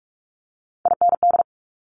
Morse code, heard as a keyed tone, about 46 wpm, 700 Hz. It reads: IND
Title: Multi-letter Phonemes Quick Fire 45wpm